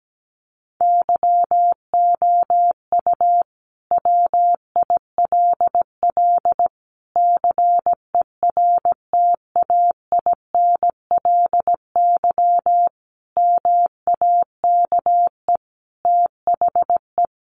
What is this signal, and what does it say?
Morse code, heard as a keyed tone, about 17 wpm, 700 Hz. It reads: YOU WILL CERTAINLY MAKE THE
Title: Sentences from Top 1400 Words 17wpm